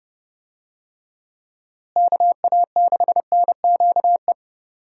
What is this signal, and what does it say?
Morse code, heard as a keyed tone, about 30 wpm, 700 Hz. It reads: KA6NQE
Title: USA Call Signs - 6 Characters - Repeated 3x - 30wpm